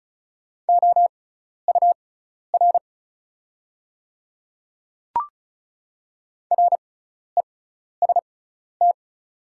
Morse code, heard as a keyed tone, about 36 wpm, spaced as for 12 wpm, 700 Hz. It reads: OUR REST
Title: 6x Character Spacing - Top 500 Words - 35wpm